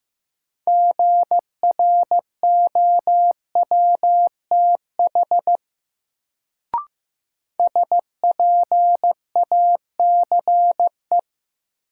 Morse code, heard as a keyed tone, about 15 wpm, 700 Hz. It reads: GROWTH SPACE
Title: Sentences from Top 1000 Words - Review Part 5 - New Word Drill - 15wpm